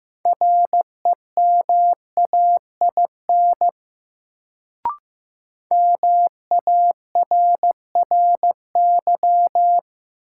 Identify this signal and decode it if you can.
Morse code, heard as a keyed tone, about 15 wpm, 700 Hz. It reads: REMAIN MARRY
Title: Sentences from Top 1200 Words - New Word Drill Alt - 15wpm